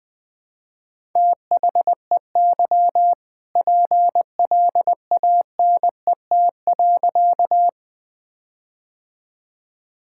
Morse code, heard as a keyed tone, about 20 wpm, 700 Hz. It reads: THEY PLANET.
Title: All-Age Appropriate Jokes - 20wpm